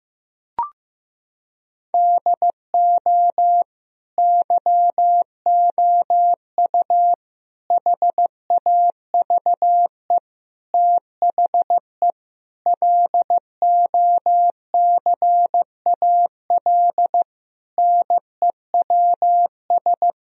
Morse code, heard as a keyed tone, about 15 wpm, 700 Hz. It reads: DO YOU HAVE THE LOCAL NEWS
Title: Sentences from Top 1000 Words - Review Part 3 - 15wpm